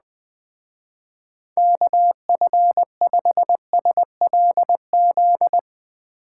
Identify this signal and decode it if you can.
Morse code, heard as a keyed tone, about 20 wpm, 700 Hz. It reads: KF5SLZ
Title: Speed Racing - USA 2x3 Call Signs 17wpm